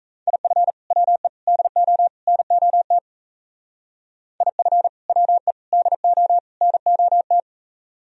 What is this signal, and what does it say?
Morse code, heard as a keyed tone, about 34 wpm, 700 Hz. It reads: IF WE DO NOT IF WE DO NOT
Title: Speed-Racing - Sets of 4 Words 28wpm